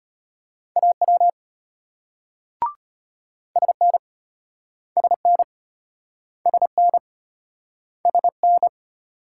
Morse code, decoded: AW SN SN SN SN

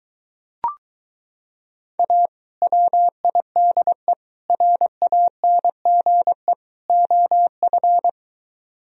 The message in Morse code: A WIDE RANGE OF

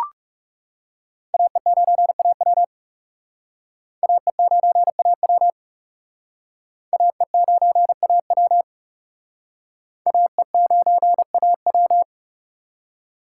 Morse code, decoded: AE9AW AE9AW AE9AW AE9AW

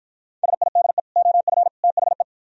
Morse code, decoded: SIDE OF THE

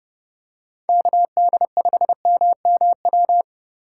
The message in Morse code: KD5MMW

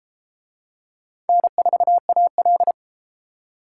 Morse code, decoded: N4AL

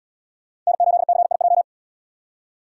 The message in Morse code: N8CIJ

